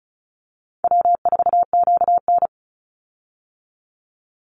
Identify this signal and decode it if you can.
Morse code, heard as a keyed tone, about 34 wpm, 700 Hz. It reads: W4QN
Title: Easy Call Signs Repeated 2x - 35wpm